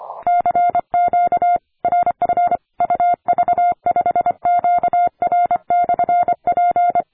Morse code, heard as a keyed tone, about 25 wpm, 700 Hz. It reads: CQ RF UV5QR/P
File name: attachments / 40m UU7JF P